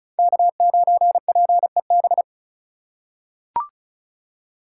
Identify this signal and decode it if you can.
Morse code, heard as a keyed tone, about 36 wpm, 700 Hz. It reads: K9PEB
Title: USA Call Signs - 5 Characters - 3x Character Spacing - 35wpm